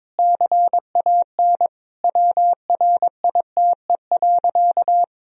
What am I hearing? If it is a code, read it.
Morse code, heard as a keyed tone, about 22 wpm, 700 Hz. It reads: CAN WRITE.
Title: Sentences from Top 1000 Words - Review Part 1 - 22wpm